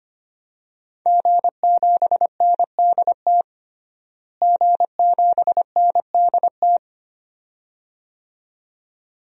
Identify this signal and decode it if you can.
Morse code, heard as a keyed tone, about 25 wpm, 700 Hz. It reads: G7NDT G7NDT
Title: DXCC-100 Call Signs - 5 Characters - Repeated 2x - 25wpm